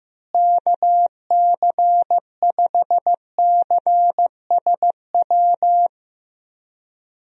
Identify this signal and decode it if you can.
Morse code, heard as a keyed tone, about 15 wpm, 700 Hz. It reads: KC5CSW
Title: USA Call Signs - 6 Characters - Repeated 3x - 15wpm